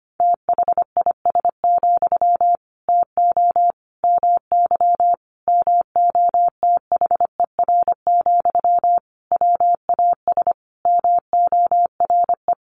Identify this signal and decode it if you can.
Morse code, heard as a keyed tone, about 25 wpm, 700 Hz. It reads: THIS, TO MY MOTHER, WAS MORE
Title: Sentences from Top 1500 Words 25wpm